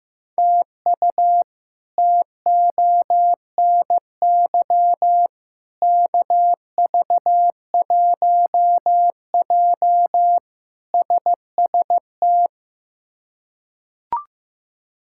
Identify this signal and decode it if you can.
Morse code, heard as a keyed tone, about 15 wpm, 700 Hz. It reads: TU TONY KV1J SST
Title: K1USN SST QSO Practice - 15wpm